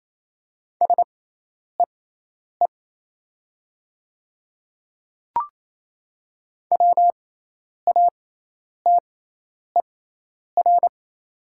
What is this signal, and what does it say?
Morse code, heard as a keyed tone, about 27 wpm, spaced as for 10 wpm, 690 Hz. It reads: SEE WATER
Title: 6x Character Spacing - Top 300 Words - 28wpm